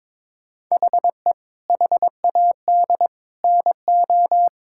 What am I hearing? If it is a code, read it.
Morse code, heard as a keyed tone, about 22 wpm, 700 Hz. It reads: HE HAD NO
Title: Sentences from Top 1600 Words 22wpm